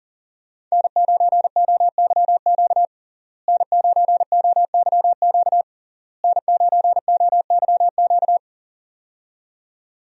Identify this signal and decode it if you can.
Morse code, heard as a keyed tone, about 40 wpm, 690 Hz. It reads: N9OYQ N9OYQ N9OYQ
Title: USA Call Signs - 5 Characters - Repeated 3x - 40wpm